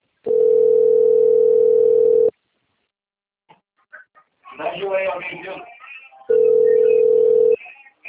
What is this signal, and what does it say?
Ringback tone (North America), 440 + 480 Hz, with speech.
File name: rg-1600-unknown- - - .49475